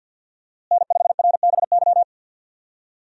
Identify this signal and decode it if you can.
Morse code, heard as a keyed tone, about 53 wpm, 700 Hz. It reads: N5RBY
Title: USA Call Signs - 5 Characters - Repeated 3x - 50wpm